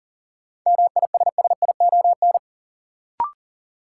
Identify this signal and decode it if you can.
Morse code, heard as a keyed tone, about 40 wpm, 700 Hz. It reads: MISSION
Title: Sentences from Top 1600 Words - New Word Drill - 40wpm